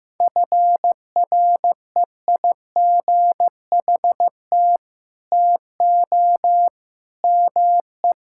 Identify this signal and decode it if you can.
Morse code, heard as a keyed tone, about 15 wpm, 700 Hz. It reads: FREIGHT TO ME
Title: Pangram Challenge v2 - 15wpm